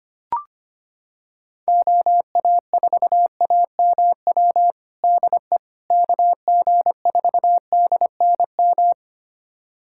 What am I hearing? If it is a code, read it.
Morse code, heard as a keyed tone, about 25 wpm, 700 Hz. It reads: OA4AMW DE KG4DNM
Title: USA Callsign DE Callsign 25wpm